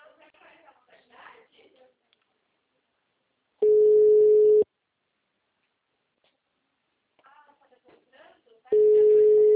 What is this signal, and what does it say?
Ringback tone (Europe, 425 Hz), with speech.